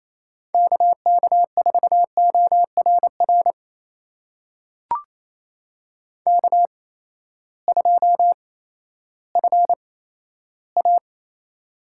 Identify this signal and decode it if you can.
Morse code, heard as a keyed tone, about 28 wpm, spaced as for 14 wpm, 700 Hz. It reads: KK4ORR K2FA
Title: USA Call Signs - 8x Character Spacing - 28wpm